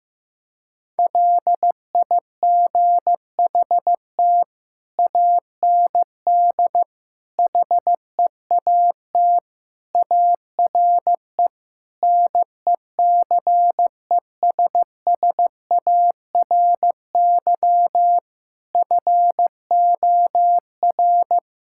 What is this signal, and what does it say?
Morse code, heard as a keyed tone, about 15 wpm, 700 Hz. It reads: LIGHT AND HEAT ARE NECESSARY FOR
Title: Sentences from Top 1000 Words - Review Part 5 - 15wpm